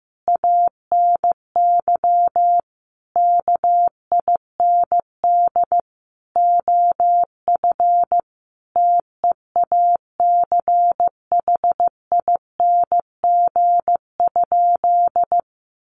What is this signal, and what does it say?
Morse code, heard as a keyed tone, about 15 wpm, 700 Hz. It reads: ANY KIND OF TEACHING?
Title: Sentences from Top 1100 Words 15wpm